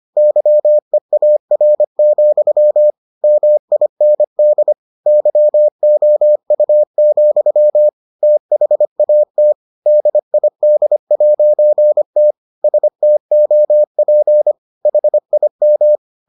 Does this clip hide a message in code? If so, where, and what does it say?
Yes: Morse code from 0.2 to 16.0 s, heard as a keyed tone, about 25 wpm, 600 Hz. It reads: YEAR, MIND YOU, THAT DID'T STOP HIM